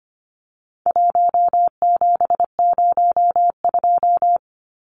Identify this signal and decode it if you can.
Morse code, heard as a keyed tone, about 25 wpm, 700 Hz. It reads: 1702
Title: Numbers - 4-Digit 25wpm